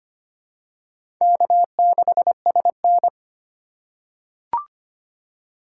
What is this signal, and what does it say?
Morse code, heard as a keyed tone, about 25 wpm, 700 Hz. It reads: K6SN
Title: USA Call Signs - 4 Characters - 3x Character Spacing - 25wpm